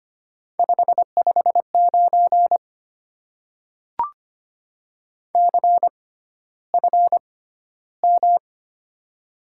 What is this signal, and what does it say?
Morse code, heard as a keyed tone, about 25 wpm, spaced as for 13 wpm, 700 Hz. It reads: 559 CFM
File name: Top 90 QSO Elements - 6x Character Spacing - 25wpm